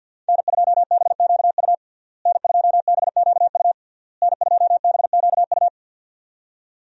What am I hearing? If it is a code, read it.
Morse code, heard as a keyed tone, about 47 wpm, 700 Hz. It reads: N2BQU N2BQU N2BQU
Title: USA Call Signs - 5 Characters - Repeated 3x - 50wpm